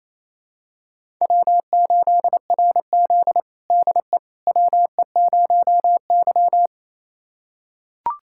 Morse code, heard as a keyed tone, about 28 wpm, 700 Hz. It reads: W8RZ DE WE0Y
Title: USA Callsign DE Callsign - 4 Characters - 28wpm